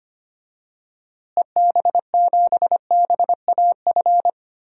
Morse code, heard as a keyed tone, about 25 wpm, 700 Hz. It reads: EB7BAF
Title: DXCC-100 Call Signs - 6 Characters - Repeated 3x - 25wpm